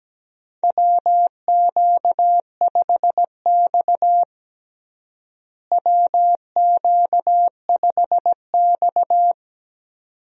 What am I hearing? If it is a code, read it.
Morse code, heard as a keyed tone, about 17 wpm, 700 Hz. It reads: WQ5X WQ5X
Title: USA Call Signs - 4 Characters - Repeated 2x - 17wpm